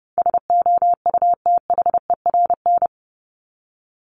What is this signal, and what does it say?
Morse code, heard as a keyed tone, about 29 wpm, 700 Hz. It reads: SOUTHERN
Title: Top 1751-2000 Words 30wpm